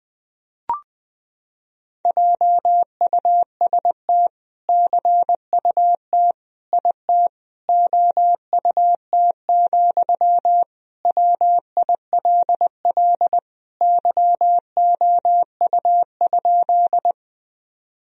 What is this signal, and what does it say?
Morse code, heard as a keyed tone, about 20 wpm, 700 Hz. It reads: JUST CUT IT OUT, WILL YOU?
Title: Sentences from Top 1000 Words - Review Part 1 - 20wpm